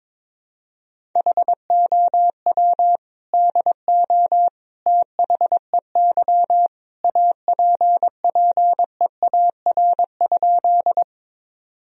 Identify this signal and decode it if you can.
Morse code, heard as a keyed tone, about 22 wpm, 700 Hz. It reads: HOW DO THEY APPEAR?
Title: Sentences from Top 1000 Words - Review Part 2 - 22wpm